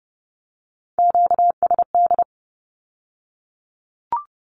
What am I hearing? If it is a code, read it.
Morse code, heard as a keyed tone, about 30 wpm, 700 Hz. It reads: QSD
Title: Q-Codes with Spoken Meaning - 30wpm